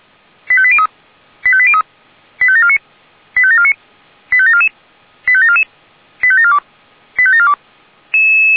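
Five-tone selective call (ZVEI), heard as quick runs of five tones, tones of about 70 ms. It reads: 85803 85803 85749 85749 85744 85744 85742 85742
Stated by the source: ZVEI five-tone call received with an SDR.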